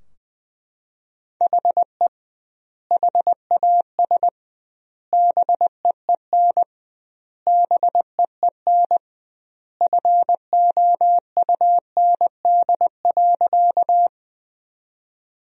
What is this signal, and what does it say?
Morse code, heard as a keyed tone, about 20 wpm, 700 Hz. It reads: HE HAS BEEN BEEN FOUND.
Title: 2x Word-Spacing - Sentences from Top 300 Words - 20wpm